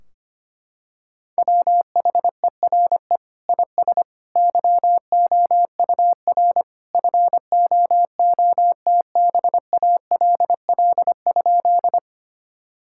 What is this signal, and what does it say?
Morse code, heard as a keyed tone, about 25 wpm, 700 Hz. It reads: WHERE IS YOUR FOOTBALL?
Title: Sentences from Top 1600 Words 25wpm